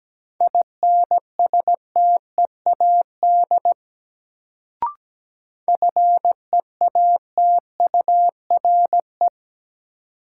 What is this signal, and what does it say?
Morse code, heard as a keyed tone, about 17 wpm, 700 Hz. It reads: INSTEAD FEATURE
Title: Sentences from Top 800 Words - New Word Drill - 17wpm